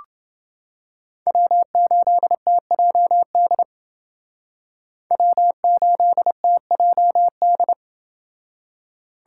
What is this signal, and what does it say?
Morse code, heard as a keyed tone, about 29 wpm, 700 Hz. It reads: W8TJD W8TJD